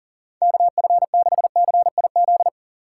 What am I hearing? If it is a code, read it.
Morse code, heard as a keyed tone, about 40 wpm, 700 Hz. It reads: KF6CIZ
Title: USA Call Signs - 6 Characters - 2x Character Spacing - 40wpm